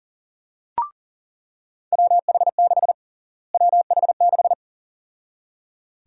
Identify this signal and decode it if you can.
Morse code, heard as a keyed tone, about 40 wpm, 700 Hz. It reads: WH6 WH6
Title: USA Call Sign Prefixes - Repeated 2x - 40wpm